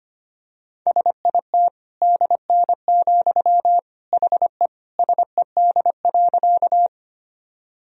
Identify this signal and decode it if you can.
Morse code, heard as a keyed tone, about 25 wpm, 700 Hz. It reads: SIT DN, HE SED.
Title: Sentences from Top 500 Words and Abbreviations 25wpm